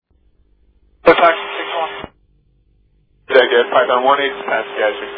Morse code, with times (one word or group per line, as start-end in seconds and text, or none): none